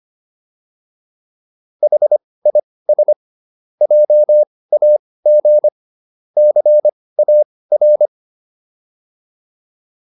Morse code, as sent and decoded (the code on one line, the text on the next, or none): .... .. ... / .--- .- --. / -.-. .- .-.
HIS JAG CAR